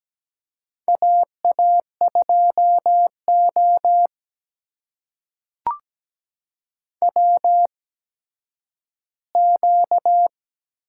Morse code, heard as a keyed tone, about 17 wpm, spaced as for 10 wpm, 700 Hz. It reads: AA2O WQ
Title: USA Call Signs - 4 Characters - 8x Character Spacing - 17wpm